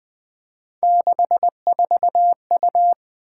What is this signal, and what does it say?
Morse code, heard as a keyed tone, about 20 wpm, 700 Hz. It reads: 64U
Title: Three Letter-Number Combinations - 20wpm